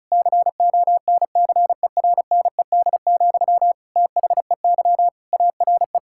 Morse code, heard as a keyed tone, about 36 wpm, 700 Hz. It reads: CONCERNED, THEY ARE
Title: Sentences from Top 1000 Words - Review Part 6 - 35wpm